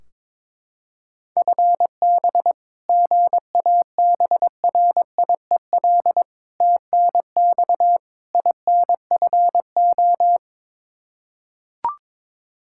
Morse code, heard as a keyed tone, about 22 wpm, 700 Hz. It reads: FB GABRIEL TNX INFO
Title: Practice QSO - Exchange 3 - More Info - 22wpm